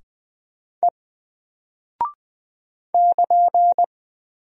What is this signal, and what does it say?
Morse code, heard as a keyed tone, about 20 wpm, 700 Hz. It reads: E <KN>
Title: Single Character (Letter:Number:Punctuation:Prosign) Alt 20wpm